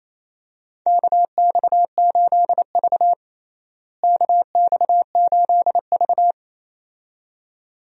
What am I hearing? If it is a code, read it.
Morse code, heard as a keyed tone, about 28 wpm, 700 Hz. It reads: KX8V KX8V
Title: USA Call Signs - 4 Characters - Repeated 2x - 28wpm